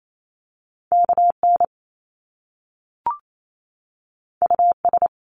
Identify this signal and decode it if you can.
Morse code, heard as a keyed tone, about 28 wpm, 700 Hz. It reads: KN US
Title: Top Two Letter Combinations in Words 28wpm